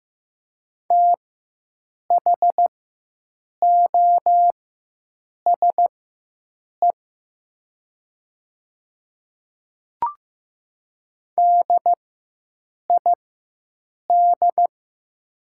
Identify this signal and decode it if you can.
Morse code, heard as a keyed tone, about 15 wpm, spaced as for 7 wpm, 700 Hz. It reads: THOSE DID
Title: 4x Character Spacing - Top 300 Words - 15wpm